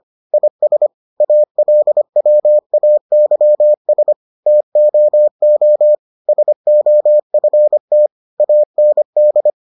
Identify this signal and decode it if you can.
Morse code, heard as a keyed tone, about 25 wpm, 600 Hz. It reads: IS ALWAYS TOO SOFT AND